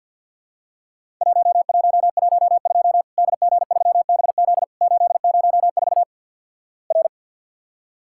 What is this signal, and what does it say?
Morse code, heard as a keyed tone, about 50 wpm, 700 Hz. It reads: 1112 DG3BZ 804 R
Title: Worked All Europe DX Contest - QTC Exchange - 50wpm